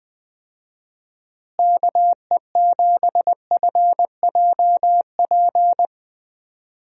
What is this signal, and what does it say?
Morse code, heard as a keyed tone, about 21 wpm, 700 Hz. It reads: KE7FJP
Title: USA Call Signs - 6 Characters - Repeated 3x - 20wpm